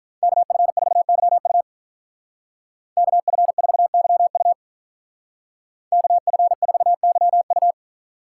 Morse code, heard as a keyed tone, about 43 wpm, 700 Hz. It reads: KF4YU KF4YU KF4YU
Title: Speed Racing - USA 2x2 Call Signs 35wpm